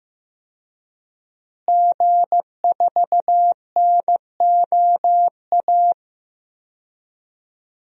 Morse code, heard as a keyed tone, about 15 wpm, 700 Hz. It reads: G4NOA
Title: DXCC-100 Call Signs - 5 Characters - Repeated 3x - 15wpm